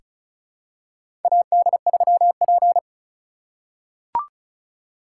Morse code, heard as a keyed tone, about 34 wpm, 700 Hz. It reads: AD3P
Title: USA Call Signs - 4 Characters - 3x Character Spacing - 35wpm